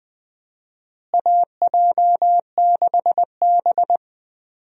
Morse code, heard as a keyed tone, about 21 wpm, 700 Hz. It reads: AJ6B
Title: USA Call Signs - 4 Characters - Repeated 3x - 20wpm